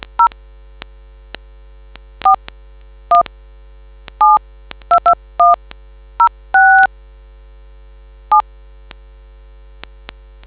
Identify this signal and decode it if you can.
DTMF, heard as telephone touch tones, tones of about 110 ms. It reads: 0 41722106 7